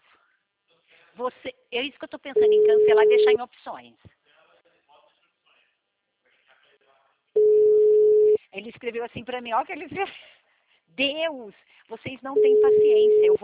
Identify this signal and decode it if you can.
Ringback tone (Europe, 425 Hz), with speech.